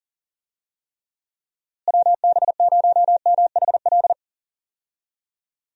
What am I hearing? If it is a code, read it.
Morse code, heard as a keyed tone, about 40 wpm, 700 Hz. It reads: WB0MHL
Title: USA Call Signs - 6 Characters - Repeated 2x - 40wpm